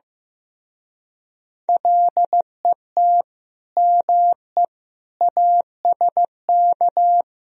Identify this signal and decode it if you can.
Morse code, heard as a keyed tone, about 16 wpm, 700 Hz. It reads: LET ME ASK
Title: Speed-Racing - Sets of 4 Words 15wpm